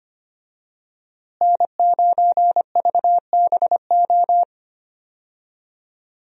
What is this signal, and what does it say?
Morse code, heard as a keyed tone, about 25 wpm, 700 Hz. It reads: N9VBO
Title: USA Call Signs - 5 Characters - Repeated 3x - 25wpm